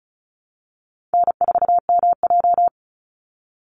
Morse code, heard as a keyed tone, about 34 wpm, 700 Hz. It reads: N4MJ
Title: NA SOTA Chaser Callsigns 35wpm